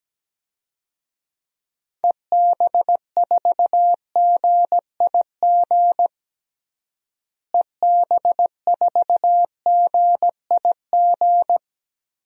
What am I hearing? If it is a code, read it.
Morse code, heard as a keyed tone, about 17 wpm, 700 Hz. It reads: EB4GIG EB4GIG
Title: DXCC-100 Call Signs - 6 Characters - Repeated 2x - 17wpm